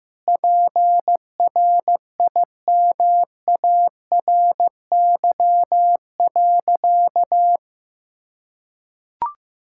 Morse code, heard as a keyed tone, about 15 wpm, 700 Hz. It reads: PRIMARY.
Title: Sentences from Top 1000 Words - Review Part 5 - 15wpm